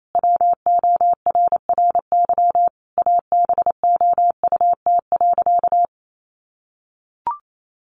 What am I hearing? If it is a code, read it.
Morse code, heard as a keyed tone, about 28 wpm, 700 Hz. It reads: WORRY ABOUT.
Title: Sentences from Top 1100 Words 28wpm